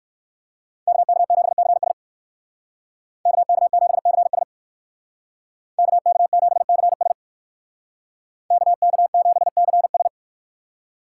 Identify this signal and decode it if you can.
Morse code, heard as a keyed tone, about 56 wpm, 700 Hz. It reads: KK7CS KK7CS KK7CS KK7CS
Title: Speed Racing - USA Call Signs 45wpm